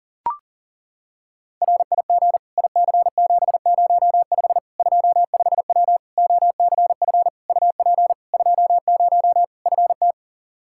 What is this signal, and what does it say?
Morse code, heard as a keyed tone, about 40 wpm, 700 Hz. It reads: RIG IC705 25W OCF UP 20 FT
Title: K1USN SST QSO Practice 40wpm